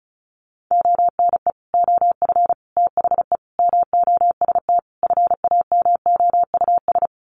Morse code, heard as a keyed tone, about 34 wpm, 700 Hz. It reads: ONE OF THE MOST FAMOUS
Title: Sets of 5 Words Encore2 35wpm